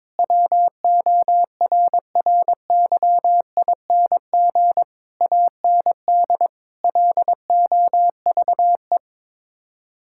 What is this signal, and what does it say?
Morse code, heard as a keyed tone, about 22 wpm, 700 Hz. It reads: WORRYING AND LOVE
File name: practice / 22wpm / Sets of 8 Words - 22wpm